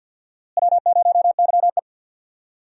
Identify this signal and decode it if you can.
Morse code, heard as a keyed tone, about 51 wpm, 700 Hz. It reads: W0YE
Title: USA Call Signs - 4 Characters - Repeated 3x - 50wpm